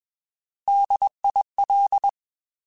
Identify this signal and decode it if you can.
Morse code, heard as a keyed tone, about 21 wpm, 800 Hz. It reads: DIL
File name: Alphabetical / Alphabetical mp3s / Ds @20